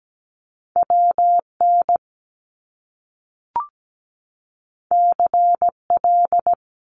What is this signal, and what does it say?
Morse code, heard as a keyed tone, about 17 wpm, 700 Hz. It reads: WN CL
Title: Top Two Letter Combinations in Words 17wpm